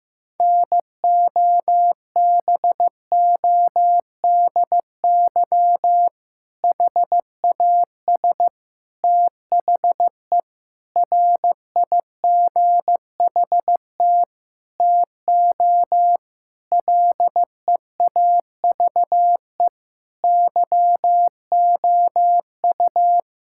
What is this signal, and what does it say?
Morse code, heard as a keyed tone, about 15 wpm, 700 Hz. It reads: NOBODY HAS THE RIGHT TO LEAVE YOU
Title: Sentences from Top 1600 Words 15wpm